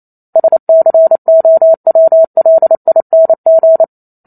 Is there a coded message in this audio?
Yes: Morse code from 0.4 to 3.8 s, heard as a keyed tone, about 28 wpm, 650 Hz. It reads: SCOWLING